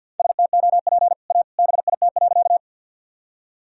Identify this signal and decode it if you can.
Morse code, heard as a keyed tone, about 52 wpm, 700 Hz. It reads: STOP A BIT.
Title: Sentences from Top 1000 Words - Review Part 1 - 50wpm